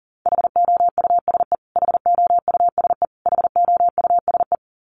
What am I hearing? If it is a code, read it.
Morse code, heard as a keyed tone, about 40 wpm, 700 Hz. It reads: HOUSE HOUSE HOUSE
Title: Top 300 Words Repeated 40wpm